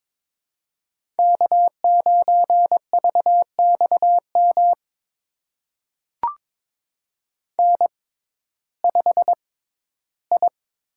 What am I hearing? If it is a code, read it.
Morse code, heard as a keyed tone, about 21 wpm, spaced as for 13 wpm, 700 Hz. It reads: K9VXM N5I
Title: USA Call Signs - 5 Characters - 6x Character Spacing - 22wpm